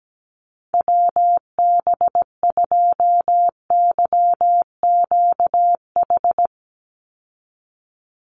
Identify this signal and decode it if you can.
Morse code, heard as a keyed tone, about 17 wpm, 700 Hz. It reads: WB2YQH
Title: Call Signs Repeated 2x - 17wpm